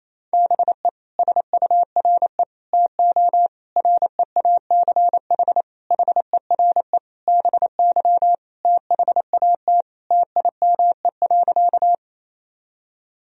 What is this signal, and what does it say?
Morse code, heard as a keyed tone, about 28 wpm, 700 Hz. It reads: BE SURE TO REACH HERE BY THAT TIME.
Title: Sentences from Top 1000 Words - Review Part 2 - 28wpm